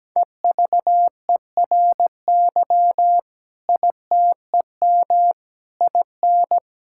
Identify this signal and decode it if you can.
Morse code, heard as a keyed tone, about 17 wpm, 700 Hz. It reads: EVERY ITEM IN
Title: Sentences from Top 1000 Words - Review Part 6 - 17wpm